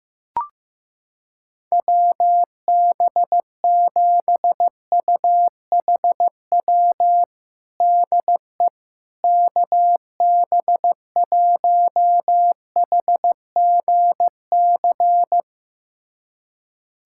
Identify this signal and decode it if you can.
Morse code, heard as a keyed tone, about 15 wpm, 700 Hz. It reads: WB7UHW DE KB1HGC
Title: USA Callsign DE Callsign 15wpm